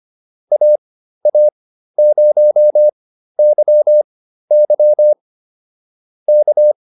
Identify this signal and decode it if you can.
Morse code, heard as a keyed tone, about 25 wpm, spaced as for 13 wpm, 600 Hz. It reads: AA0YY K